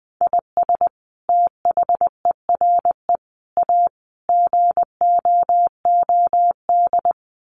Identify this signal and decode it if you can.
Morse code, heard as a keyed tone, about 20 wpm, 700 Hz. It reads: IS THERE A GOOD
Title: Sentences from Top 1400 Words 20wpm